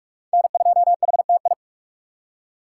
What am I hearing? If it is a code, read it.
Morse code, heard as a keyed tone, about 56 wpm, 700 Hz. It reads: N2HTI
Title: Speed Racing - USA Call Signs 45wpm